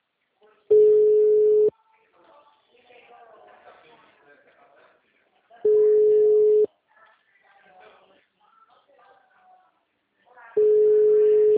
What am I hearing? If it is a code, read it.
Ringback tone (Europe, 425 Hz), with speech.